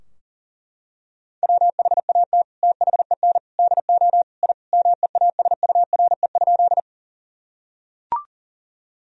Morse code, heard as a keyed tone, about 40 wpm, 700 Hz. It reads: WHAT THEN DO I MEASURE?
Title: Sentences from Top 1000 Words - Review Part 2 - 40wpm